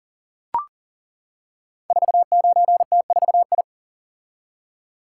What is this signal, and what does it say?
Morse code, heard as a keyed tone, about 40 wpm, 700 Hz. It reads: 49T4I
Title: Code Groups - 5 Letter:Numbers - 40wpm